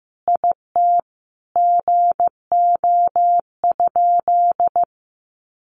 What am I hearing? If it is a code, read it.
Morse code, heard as a keyed tone, about 15 wpm, 700 Hz. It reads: IT GO?
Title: Sentences from Top 200 Words 15wpm